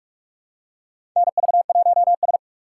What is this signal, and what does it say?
Morse code, heard as a keyed tone, about 47 wpm, 700 Hz. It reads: NV1S
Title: USA Call Signs - 4 Characters - 6x Character Spacing - 45wpm